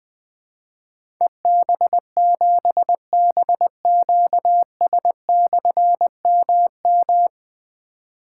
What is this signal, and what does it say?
Morse code, heard as a keyed tone, about 20 wpm, 700 Hz. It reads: EB7BQS/MM
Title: Difficult Call Signs 20wpm